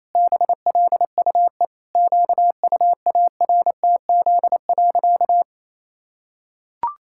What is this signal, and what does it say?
Morse code, heard as a keyed tone, about 28 wpm, 700 Hz. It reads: BLUE QUARTZ.
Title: Pangram Challenge v2 - 28wpm